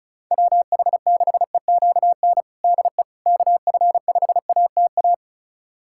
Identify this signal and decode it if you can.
Morse code, heard as a keyed tone, about 36 wpm, 700 Hz. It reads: WH6EQN DE KF5ATA
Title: USA Callsign DE Callsign - 6 Characters - 35wpm